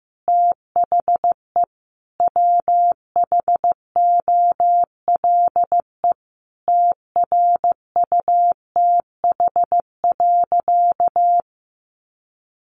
Morse code, heard as a keyed tone, about 15 wpm, 700 Hz. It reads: THE WHOLE TRUTH.
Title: Sentences from Top 800 Words 15wpm